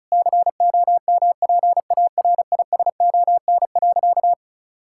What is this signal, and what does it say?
Morse code, heard as a keyed tone, about 36 wpm, 700 Hz. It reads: COMPARISON.
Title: Sentences from Top 1600 Words 35wpm